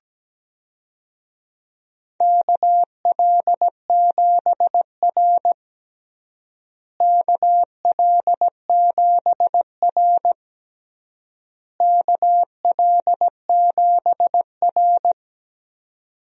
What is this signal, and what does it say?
Morse code, heard as a keyed tone, about 17 wpm, 700 Hz. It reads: KL7R KL7R KL7R
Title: DXCC-100 Call Signs - 4 Characters - Repeated 3x - 17wpm